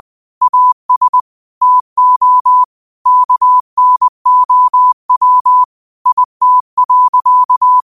Morse code, heard as a keyed tone, about 20 wpm, 1000 Hz. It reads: AS TO KNOW IT.
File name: qotdfiles / 20wpm